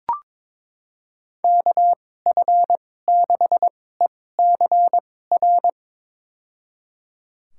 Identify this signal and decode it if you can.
Morse code, heard as a keyed tone, about 22 wpm, spaced as for 16 wpm, 700 Hz. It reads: KF6ECR
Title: USA Call Signs - 6 Characters - 2x Character Spacing - 22wpm